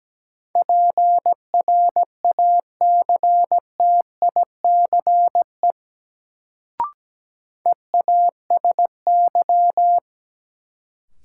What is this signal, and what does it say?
Morse code, heard as a keyed tone, about 17 wpm, 700 Hz. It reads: PRACTICE EASY
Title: Sentences from Top 700 Words - New Word Drill - 17wpm